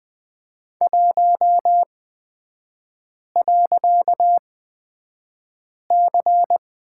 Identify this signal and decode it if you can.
Morse code, heard as a keyed tone, about 20 wpm, spaced as for 5 wpm, 700 Hz. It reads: 1.C
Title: 255 Lesson 41e - Period - Three Characters 20wpm